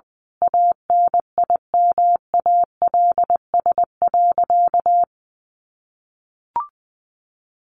Morse code, heard as a keyed tone, about 20 wpm, 700 Hz. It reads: ANIMALS.
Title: A Sound Of Thunder 20wpm